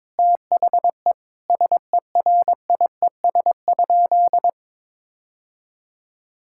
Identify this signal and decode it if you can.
Morse code, heard as a keyed tone, about 22 wpm, 700 Hz. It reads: THE SERIES?
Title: Sentences from Top 1000 Words - Review Part 5 - 22wpm